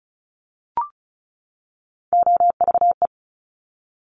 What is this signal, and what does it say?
Morse code, heard as a keyed tone, about 36 wpm, 700 Hz. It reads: OVE
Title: Top Three Letter Combinations in Words 35wpm